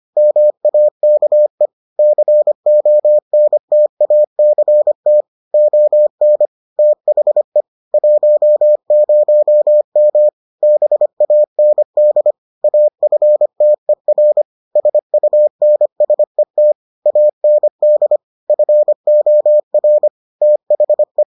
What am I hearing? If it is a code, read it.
Morse code, heard as a keyed tone, about 25 wpm, 600 Hz. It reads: MAKE CONTACT ON THE 10M BAND AFTER SUNSET AND FOR THE